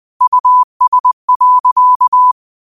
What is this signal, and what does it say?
Morse code, heard as a keyed tone, about 20 wpm, 1000 Hz. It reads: US.
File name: qotdfiles / 20wpm